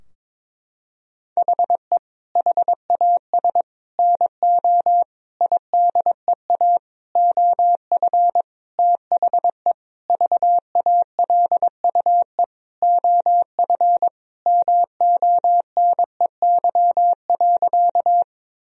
Morse code, heard as a keyed tone, about 22 wpm, 700 Hz. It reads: HE HAS NO IDEA OF THE VALUE OF MONEY.
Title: Sentences from Top 1000 Words - Review Part 3 - 22wpm